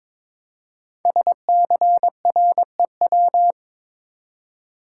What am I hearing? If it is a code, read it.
Morse code, heard as a keyed tone, about 22 wpm, 700 Hz. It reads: SCREW
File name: 5 Letter Words - First 3 Letters Given - 22wpm